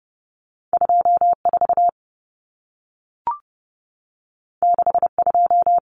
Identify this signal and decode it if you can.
Morse code, heard as a keyed tone, about 30 wpm, 700 Hz. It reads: 24 62
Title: Common Cut Numbers - 2-Digit 30wpm